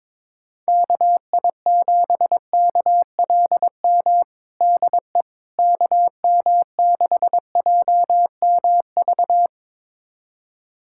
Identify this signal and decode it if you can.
Morse code, heard as a keyed tone, about 21 wpm, 700 Hz. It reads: KI7KLM DE KM6JMV
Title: USA Callsign DE Callsign - 6 Characters - 22wpm